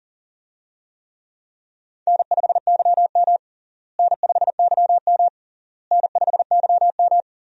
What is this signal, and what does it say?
Morse code, heard as a keyed tone, about 37 wpm, 700 Hz. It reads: N5YM N5YM N5YM
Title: USA Call Signs - 4 Characters - Repeated 3x - 40wpm